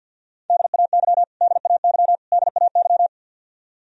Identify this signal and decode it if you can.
Morse code, heard as a keyed tone, about 48 wpm, 700 Hz. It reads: DAY DAY DAY
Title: Top 100 Words Repeated 50wpm